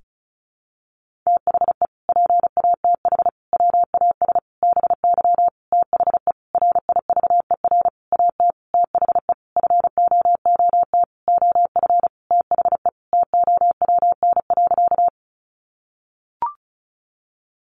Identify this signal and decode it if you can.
Morse code, heard as a keyed tone, about 34 wpm, 700 Hz. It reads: THE PATH WAS BY THE RIVER AT THE FOOT OF THE TOWN.
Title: Sentences from Top 1400 Words 35wpm